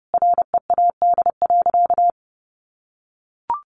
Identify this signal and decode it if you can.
Morse code, heard as a keyed tone, about 30 wpm, 700 Hz. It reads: READ.
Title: Sentences from Top 1400 Words 30wpm